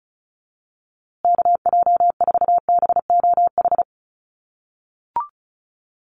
Morse code, heard as a keyed tone, about 34 wpm, 700 Hz. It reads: KJ4BOH
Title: USA Call Signs Repeated 2x 35wpm